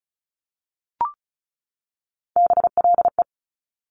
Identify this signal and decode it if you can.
Morse code, heard as a keyed tone, about 36 wpm, 690 Hz. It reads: BLE
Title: Top Three Letter Combinations in Words 35wpm